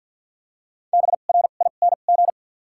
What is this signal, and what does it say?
Morse code, heard as a keyed tone, about 54 wpm, 700 Hz. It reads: B R I N G
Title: 2x Character Spacing - Top 500 Words - 50wpm